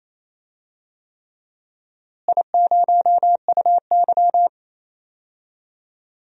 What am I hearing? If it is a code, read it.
Morse code, heard as a keyed tone, about 28 wpm, 700 Hz. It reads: I0UY
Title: DXCC-100 Call Signs - 4 Characters - Repeated 3x - 28wpm